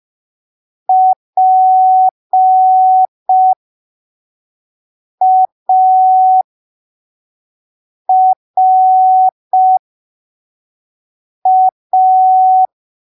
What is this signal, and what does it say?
Morse code, heard as a keyed tone, about 5 wpm, spaced as for 3 wpm, 750 Hz. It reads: PARA
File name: arquivos / texto / texto 24 05ppm